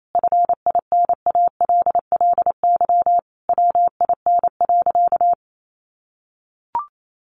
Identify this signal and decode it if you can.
Morse code, heard as a keyed tone, about 28 wpm, 700 Hz. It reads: FINALLY WIN.
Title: Sentences from Top 1500 Words 28wpm